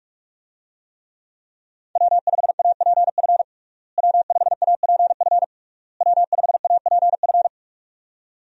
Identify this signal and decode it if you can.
Morse code, heard as a keyed tone, about 42 wpm, 700 Hz. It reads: W5APF W5APF W5APF
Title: USA Call Signs - 5 Characters - Repeated 3x - 45wpm